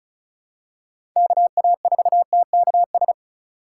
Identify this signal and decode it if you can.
Morse code, heard as a keyed tone, about 34 wpm, 700 Hz. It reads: KA4TKS
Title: USA Call Signs - 2x Character Spacing - 35wpm